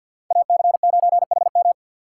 Morse code, heard as a keyed tone, about 51 wpm, 690 Hz. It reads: AC9HM
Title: USA Call Signs - 2x Character Spacing - 50wpm